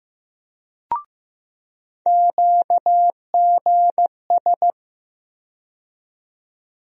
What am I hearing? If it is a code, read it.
Morse code, heard as a keyed tone, about 15 wpm, 700 Hz. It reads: QGS
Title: USA Call Sign Postfixes - 3 Letters 15wpm